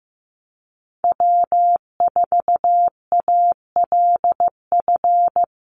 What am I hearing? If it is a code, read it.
Morse code, heard as a keyed tone, about 15 wpm, 700 Hz. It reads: W4ALF
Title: Call Signs Repeated 2x - 15wpm